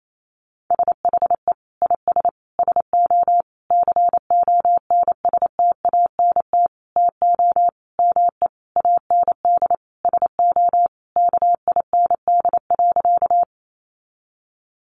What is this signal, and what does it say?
Morse code, heard as a keyed tone, about 28 wpm, 700 Hz. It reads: SHE IS SO CONSTANT TO ME AND SO KIND.
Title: Sentences from Top 1500 Words 28wpm